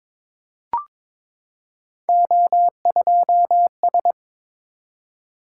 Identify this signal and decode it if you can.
Morse code, heard as a keyed tone, about 22 wpm, 700 Hz. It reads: O2S
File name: practice / 22wpm / Three Letter-Number Combinations - 22wpm